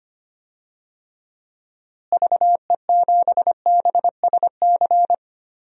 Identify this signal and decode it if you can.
Morse code, heard as a keyed tone, about 25 wpm, 700 Hz. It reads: VE7BSC
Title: DXCC-100 Call Signs - 6 Characters - Repeated 2x - 25wpm